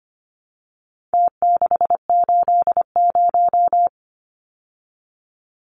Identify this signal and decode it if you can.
Morse code, heard as a keyed tone, about 25 wpm, 700 Hz. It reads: T680
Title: Numbers - Contest Style - 4-Digit 25wpm